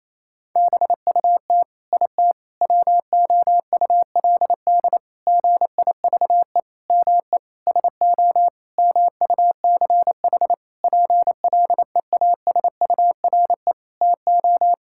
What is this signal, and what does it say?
Morse code, heard as a keyed tone, about 28 wpm, 700 Hz. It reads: BUT IT WOULD GIVE ME SO MUCH PLEASURE TO
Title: Sentences from Top 1600 Words 28wpm